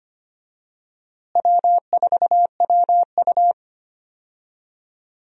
Morse code, heard as a keyed tone, about 25 wpm, 700 Hz. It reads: W4WU
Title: USA Call Signs - 4 Characters - Repeated 2x - 25wpm